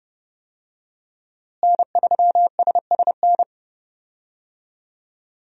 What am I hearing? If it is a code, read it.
Morse code, heard as a keyed tone, about 30 wpm, 700 Hz. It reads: N3SSN
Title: USA Call Signs - 5 Characters - Repeated 3x - 30wpm